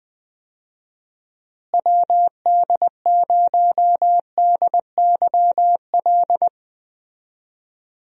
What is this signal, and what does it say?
Morse code, heard as a keyed tone, about 20 wpm, 700 Hz. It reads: WD0DYL
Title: USA Call Signs - 6 Characters - Repeated 2x - 20wpm